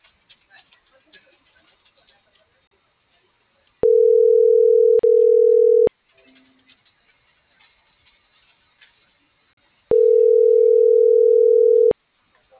Ringback tone (North America), 440 + 480 Hz.